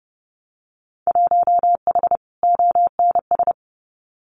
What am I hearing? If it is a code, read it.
Morse code, heard as a keyed tone, about 30 wpm, 700 Hz. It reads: 1H ONS
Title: Winter Field Day Exchanges 30wpm